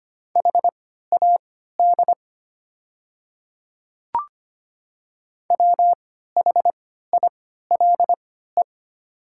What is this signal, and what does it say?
Morse code, heard as a keyed tone, about 25 wpm, spaced as for 14 wpm, 690 Hz. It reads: HAD WHILE
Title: 3x Character Spacing - Top 300 Words - 25wpm